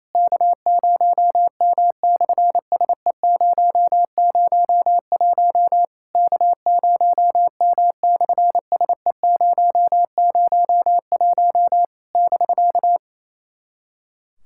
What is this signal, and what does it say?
Morse code, heard as a keyed tone, about 28 wpm, 700 Hz. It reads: K0M/SE001 K0M/SE001 <BK>
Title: SOTA QSO Practice 28wpm